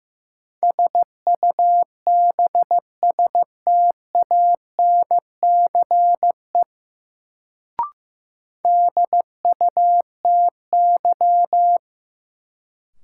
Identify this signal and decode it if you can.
Morse code, heard as a keyed tone, about 15 wpm, 700 Hz. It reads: SUBSTANCE DUTY
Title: Sentences from Top 1500 Words - New Word Drill - 15wpm